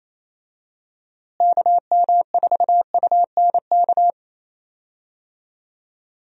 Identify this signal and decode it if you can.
Morse code, heard as a keyed tone, about 28 wpm, 700 Hz. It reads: KM4UNK
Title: USA Call Signs - 6 Characters - Repeated 2x - 28wpm